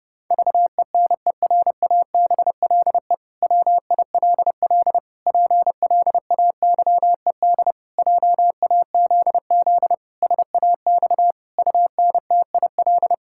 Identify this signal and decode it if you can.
Morse code, heard as a keyed tone, about 30 wpm, 700 Hz. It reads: VENERABLE WILL PLAYED JAZZ SAX UNTIL
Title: Pangram Challenge v2 - 30wpm